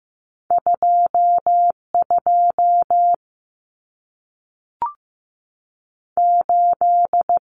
Morse code, heard as a keyed tone, about 15 wpm, 700 Hz. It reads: 2 2 8
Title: Common Cut Numbers - 2-Digit 15wpm